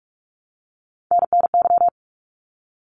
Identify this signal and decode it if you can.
Morse code, heard as a keyed tone, about 46 wpm, 700 Hz. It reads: NNY
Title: ARRL Sections 45wpm